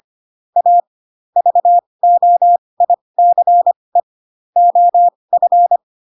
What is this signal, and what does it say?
Morse code, heard as a keyed tone, about 25 wpm, spaced as for 20 wpm, 700 Hz. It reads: A VOICE OF
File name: Pinocchio / Pinocchio ch3 15 25 ik1hge